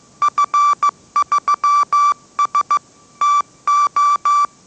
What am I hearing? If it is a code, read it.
Morse code, heard as a keyed tone, about 15 wpm, 1160 Hz. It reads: F3S TO
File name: mrx / mrx065 16wpm